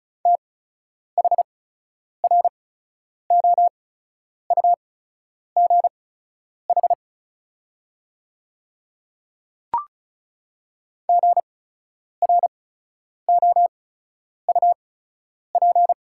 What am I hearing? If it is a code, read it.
Morse code, heard as a keyed tone, about 36 wpm, spaced as for 10 wpm, 690 Hz. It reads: THROUGH GROUP
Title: 8x Character Spacing - Top 300 Words - 35wpm